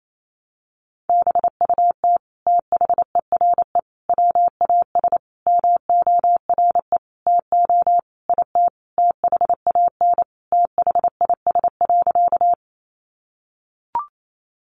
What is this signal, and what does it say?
Morse code, heard as a keyed tone, about 28 wpm, 700 Hz. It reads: BUT THERE WAS MORE TO IT THAN THIS.
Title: Sentences from Top 100 Words 28wpm